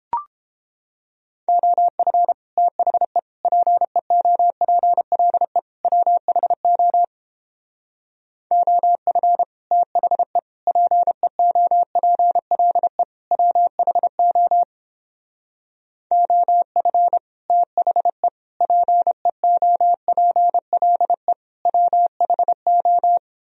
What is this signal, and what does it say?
Morse code, heard as a keyed tone, about 31 wpm, 700 Hz. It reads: OF THE PEOPLE WHO OF THE PEOPLE WHO OF THE PEOPLE WHO